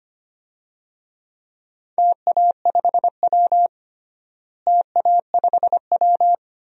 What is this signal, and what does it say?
Morse code, heard as a keyed tone, about 25 wpm, 700 Hz. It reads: TA5W TA5W
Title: DXCC-100 Call Signs - 4 Characters - Repeated 2x - 25wpm